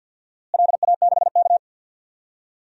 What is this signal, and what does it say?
Morse code, heard as a keyed tone, about 52 wpm, 700 Hz. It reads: LA6K
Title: DXCC-100 Call Signs - 4 Characters - Repeated 3x - 50wpm